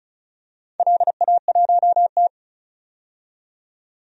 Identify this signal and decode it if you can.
Morse code, heard as a keyed tone, about 36 wpm, 700 Hz. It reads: LA1T
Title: DXCC-100 Call Signs - 4 Characters - 35wpm